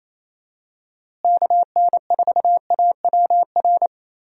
Morse code, heard as a keyed tone, about 28 wpm, 700 Hz. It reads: KN4AWR
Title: USA Call Signs - 6 Characters - Repeated 2x - 28wpm